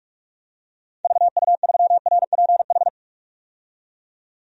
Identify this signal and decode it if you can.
Morse code, heard as a keyed tone, about 44 wpm, 700 Hz. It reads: VU3RPH
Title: DXCC-100 Call Signs - 6 Characters - 45wpm